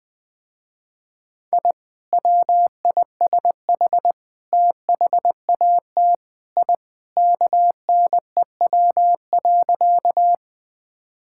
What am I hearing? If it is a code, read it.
Morse code, heard as a keyed tone, about 20 wpm, 700 Hz. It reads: I WISH THAT I KNEW.
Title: Sentences with 2-Word Phrases - Spoken Beginning - 20wpm